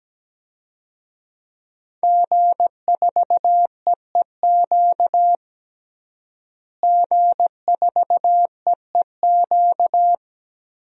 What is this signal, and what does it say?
Morse code, heard as a keyed tone, about 17 wpm, 700 Hz. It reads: G4EEQ G4EEQ
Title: DXCC-100 Call Signs - 5 Characters - Repeated 2x - 17wpm